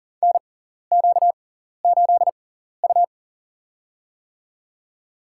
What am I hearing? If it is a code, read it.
Morse code, heard as a keyed tone, about 40 wpm, spaced as for 14 wpm, 700 Hz. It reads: NQ8U
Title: USA Call Signs - 4 Characters - 6x Character Spacing - 40wpm